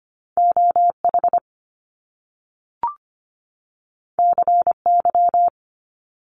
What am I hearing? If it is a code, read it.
Morse code, heard as a keyed tone, about 25 wpm, 700 Hz. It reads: OH CY
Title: Top Two Letter Combinations in Words 25wpm